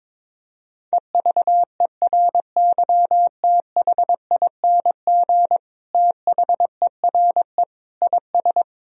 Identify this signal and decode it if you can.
Morse code, heard as a keyed tone, about 22 wpm, 700 Hz. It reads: EVERYTHING THERE IS
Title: Sentences from Top 1000 Words - Review Part 1 - 22wpm